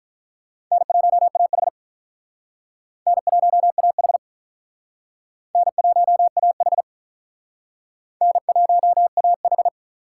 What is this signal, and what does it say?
Morse code, heard as a keyed tone, about 45 wpm, 700 Hz. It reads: N1AH N1AH N1AH N1AH